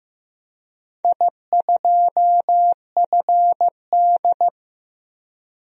Morse code, heard as a keyed tone, about 15 wpm, 700 Hz. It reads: I2FD
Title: DXCC-100 Call Signs - 4 Characters - Repeated 3x - 15wpm